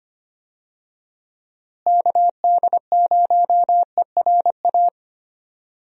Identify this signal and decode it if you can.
Morse code, heard as a keyed tone, about 25 wpm, 700 Hz. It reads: KD0ERA
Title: USA Call Signs - 6 Characters - Repeated 2x - 25wpm